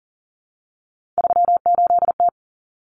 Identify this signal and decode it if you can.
Morse code, heard as a keyed tone, about 40 wpm, 700 Hz. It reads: 38T
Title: Common Cut Numbers - 3-Digit 40wpm